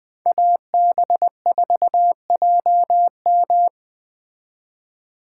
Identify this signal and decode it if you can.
Morse code, heard as a keyed tone, about 20 wpm, 700 Hz. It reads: AB4JM
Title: Speed Racing - USA 2x2 Call Signs 17wpm